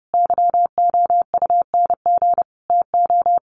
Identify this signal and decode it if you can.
Morse code, heard as a keyed tone, about 30 wpm, 700 Hz. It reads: YOUNG TO
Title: Sentences from Top 600 Words 30wpm